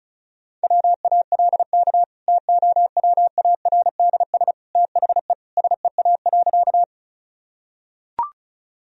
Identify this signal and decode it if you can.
Morse code, heard as a keyed tone, about 36 wpm, 700 Hz. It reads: WALK TOWARDS THE SEA.
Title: Sentences from Top 1000 Words - Review Part 2 - 35wpm